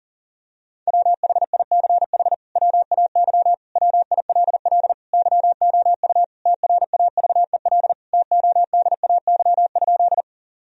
Morse code, heard as a keyed tone, about 40 wpm, 700 Hz. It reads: WHICH WAY WILL YOU TRAVEL TODAY?
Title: Sentences from Top 1000 Words - Review Part 2 - 40wpm